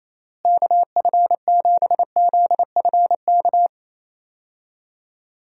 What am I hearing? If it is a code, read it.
Morse code, heard as a keyed tone, about 28 wpm, 700 Hz. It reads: KF7ZFK
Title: USA Call Signs - 6 Characters - Repeated 3x - 28wpm